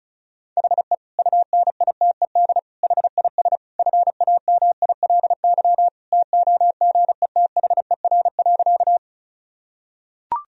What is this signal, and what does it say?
Morse code, heard as a keyed tone, about 34 wpm, 700 Hz. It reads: HE UNITED HIS FAMILY TOGETHER.
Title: Sentences from Top 1000 Words - Review Part 4 - 35wpm